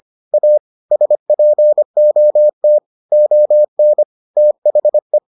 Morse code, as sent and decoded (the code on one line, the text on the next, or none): .- / ... .--. --- - / --- -. / - .... .
A SPOT ON THE